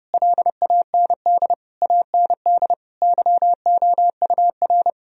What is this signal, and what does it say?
Morse code, heard as a keyed tone, about 30 wpm, 700 Hz. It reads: LAND AND YOUR
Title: Sentences from Top 1000 Words - Review Part 4 - 30wpm